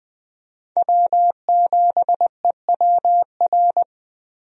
Morse code, heard as a keyed tone, about 21 wpm, 700 Hz. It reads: W7EWR
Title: USA Call Signs - 5 Characters - Repeated 2x - 20wpm